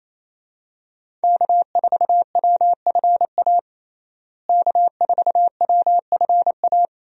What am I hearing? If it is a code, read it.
Morse code, heard as a keyed tone, about 28 wpm, 700 Hz. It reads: K4WFA K4WFA
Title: USA Call Signs - 5 Characters - Repeated 2x - 28wpm